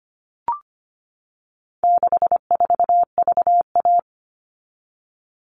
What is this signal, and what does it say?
Morse code, heard as a keyed tone, about 25 wpm, 700 Hz. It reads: 64VA
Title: All Cut Numbers - 4-Digit 25wpm